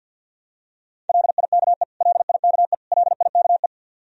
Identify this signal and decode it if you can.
Morse code, heard as a keyed tone, about 52 wpm, 700 Hz. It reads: LIKE LIKE LIKE
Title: Top 100 Words Repeated 50wpm